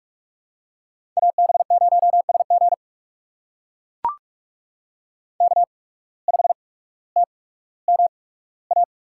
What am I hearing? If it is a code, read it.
Morse code, heard as a keyed tone, about 44 wpm, spaced as for 19 wpm, 700 Hz. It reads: AB0SG K5TMA
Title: USA Call Signs - 5 Characters - 8x Character Spacing - 45wpm